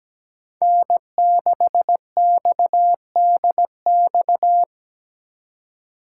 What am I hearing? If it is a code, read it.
Morse code, heard as a keyed tone, about 17 wpm, 700 Hz. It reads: N6XDX
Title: USA Call Signs - 5 Characters - 4x Character Spacing - 17wpm